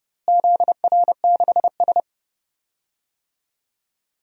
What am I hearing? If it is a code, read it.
Morse code, heard as a keyed tone, about 31 wpm, 700 Hz. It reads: ZR6S
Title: DXCC-100 Call Signs - 4 Characters - Repeated 2x - 30wpm